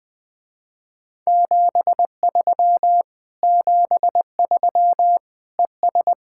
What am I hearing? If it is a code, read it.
Morse code, heard as a keyed tone, about 20 wpm, 700 Hz. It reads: 73 73 ES
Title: Speed-Racing Common QSO Phrases - 17wpm